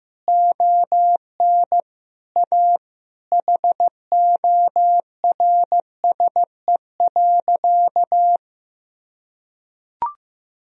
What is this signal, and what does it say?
Morse code, heard as a keyed tone, about 15 wpm, 700 Hz. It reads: ON A HORSE.
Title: Sentences from Top 1000 Words - Review Part 1 - 15wpm